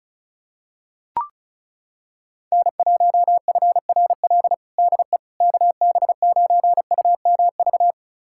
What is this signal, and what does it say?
Morse code, heard as a keyed tone, about 34 wpm, 700 Hz. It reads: N1FRL DE KB9UMV
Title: USA Callsign DE Callsign 35wpm